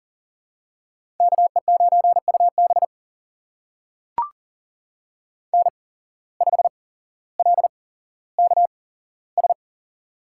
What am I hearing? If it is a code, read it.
Morse code, heard as a keyed tone, about 40 wpm, spaced as for 17 wpm, 700 Hz. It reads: KE9UB N5LKS
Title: USA Call Signs - 5 Characters - 8x Character Spacing - 40wpm